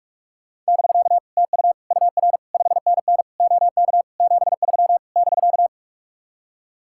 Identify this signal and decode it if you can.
Morse code, heard as a keyed tone, about 59 wpm, spaced as for 52 wpm, 700 Hz. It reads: <BK> TU UR 5NN OK 73 <BK>
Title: Speed-Racing Common QSO Phrases Encore - 45wpm